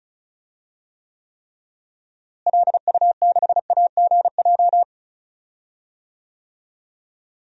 Morse code, heard as a keyed tone, about 36 wpm, 700 Hz. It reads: LU6AGJ
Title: DXCC-100 Call Signs - 6 Characters - Repeated 3x - 35wpm